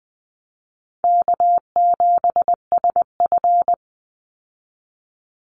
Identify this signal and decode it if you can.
Morse code, heard as a keyed tone, about 20 wpm, 700 Hz. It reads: K7SF
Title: Easy Call Signs Repeated 3x - 20wpm